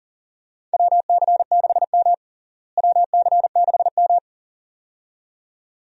Morse code, heard as a keyed tone, about 40 wpm, 700 Hz. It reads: WC6M WC6M
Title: USA Call Signs - 4 Characters - Repeated 2x - 40wpm